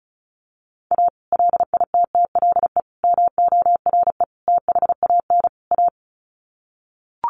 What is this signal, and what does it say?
Morse code, heard as a keyed tone, about 34 wpm, 700 Hz. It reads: A LITTLE MORE THAN A
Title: Sets of 5 Words Encore2 35wpm